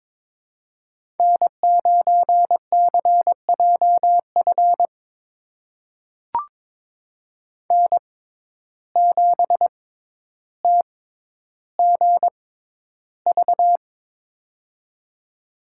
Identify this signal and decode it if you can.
Morse code, heard as a keyed tone, about 22 wpm, spaced as for 11 wpm, 700 Hz. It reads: N9CJF N7TGV
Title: USA Call Signs - 5 Characters - 6x Character Spacing - 22wpm